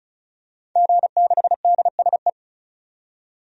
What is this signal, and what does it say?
Morse code, heard as a keyed tone, about 34 wpm, 700 Hz. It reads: G6DSE
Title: DXCC-100 Call Signs - 5 Characters - 35wpm